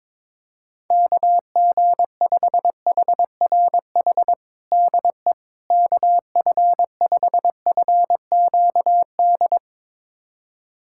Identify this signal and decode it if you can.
Morse code, heard as a keyed tone, about 22 wpm, 700 Hz. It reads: KG5HRH DE KF5FQD
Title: USA Callsign DE Callsign - 6 Characters - 22wpm